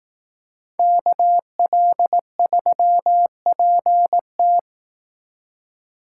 Morse code, heard as a keyed tone, about 18 wpm, 700 Hz. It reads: KL3PT
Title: Speed Racing - USA 2x2 Call Signs 15wpm